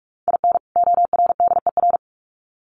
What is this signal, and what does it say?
Morse code, heard as a keyed tone, about 44 wpm, 700 Hz. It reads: IN ORDER
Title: Sets of 2 Words 45wpm